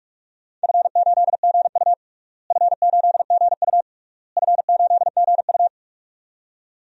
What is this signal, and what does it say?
Morse code, heard as a keyed tone, about 45 wpm, 700 Hz. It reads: F8GU F8GU F8GU
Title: DXCC-100 Call Signs - 4 Characters - Repeated 3x - 45wpm